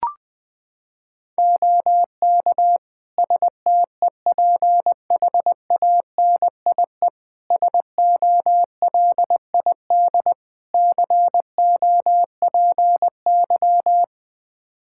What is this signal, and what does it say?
Morse code, heard as a keyed tone, about 20 wpm, 700 Hz. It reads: OK STEPHANIE SOLID COPY
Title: Practice QSO - Exchange 3 - More Info - 20wpm